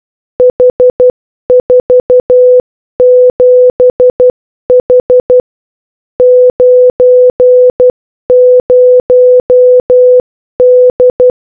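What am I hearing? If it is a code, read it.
Morse code, heard as a keyed tone, about 12 wpm, 500 Hz. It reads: H47H 90D
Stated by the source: Morse code puzzle audio from a capture-the-flag cryptography challenge.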